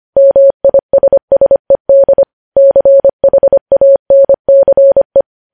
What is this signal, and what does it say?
Morse code, heard as a keyed tone, about 25 wpm, 550 Hz. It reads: MISSED CHANCE
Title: News Headlines at 25